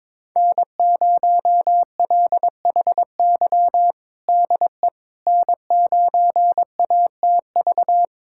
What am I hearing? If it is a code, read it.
Morse code, heard as a keyed tone, about 22 wpm, 700 Hz. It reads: N0LHY DE N9ATV
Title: USA Callsign DE Callsign - 5 Characters - 22wpm